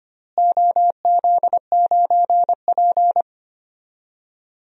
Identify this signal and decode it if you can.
Morse code, heard as a keyed tone, about 25 wpm, 700 Hz. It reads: OZ9P
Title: DXCC-100 Call Signs - 4 Characters - Repeated 3x - 25wpm